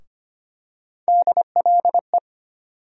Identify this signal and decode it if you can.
Morse code, heard as a keyed tone, about 25 wpm, 700 Hz. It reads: DLE
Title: USA Call Sign Postfixes - 3 Letters 25wpm